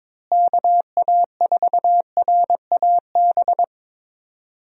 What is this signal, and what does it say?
Morse code, heard as a keyed tone, about 22 wpm, 700 Hz. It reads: KA4RAB
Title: USA Call Signs - 6 Characters - Repeated 2x - 22wpm